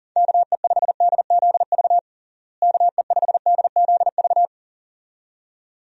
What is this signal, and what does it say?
Morse code, heard as a keyed tone, about 40 wpm, 700 Hz. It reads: KE5DZV KE5DZV
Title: USA Call Signs - 6 Characters - Repeated 2x - 40wpm